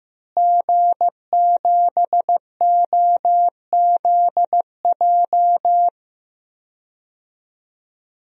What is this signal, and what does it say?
Morse code, heard as a keyed tone, about 15 wpm, 700 Hz. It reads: G7OZJ
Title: DXCC-100 Call Signs - 5 Characters - Repeated 3x - 15wpm